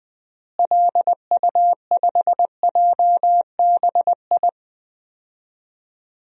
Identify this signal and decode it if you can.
Morse code, heard as a keyed tone, about 20 wpm, 700 Hz. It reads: LU5JBI
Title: DXCC-100 Call Signs - 6 Characters - Repeated 3x - 20wpm